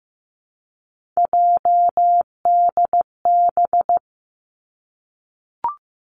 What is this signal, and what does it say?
Morse code, heard as a keyed tone, about 15 wpm, 700 Hz. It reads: JDB
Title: Three Challenging Letter-Number Combinations 15wpm